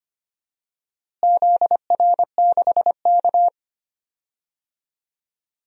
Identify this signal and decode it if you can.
Morse code, heard as a keyed tone, about 25 wpm, 700 Hz. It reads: ZR6K
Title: DXCC-100 Call Signs - 4 Characters - Repeated 3x - 25wpm